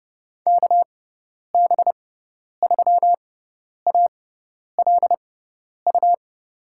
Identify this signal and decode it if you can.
Morse code, heard as a keyed tone, about 30 wpm, spaced as for 10 wpm, 700 Hz. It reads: KB3ALU
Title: USA Call Signs - 6 Characters - 6x Character Spacing - 30wpm